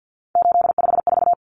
Morse code, heard as a keyed tone, about 50 wpm, 700 Hz. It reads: 854
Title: Common Cut Numbers - 3-Digit 50wpm